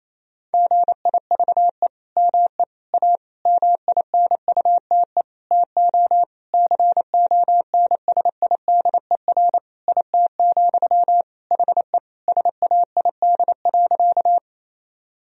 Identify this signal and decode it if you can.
Morse code, heard as a keyed tone, about 28 wpm, 700 Hz. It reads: GIVE ME A MINUTE TO CONSIDER IT, HE SAID.
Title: Sentences from Top 1000 Words - Review Part 4 - 28wpm